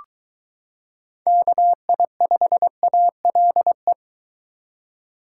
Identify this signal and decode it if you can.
Morse code, heard as a keyed tone, about 22 wpm, 700 Hz. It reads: KI5ALE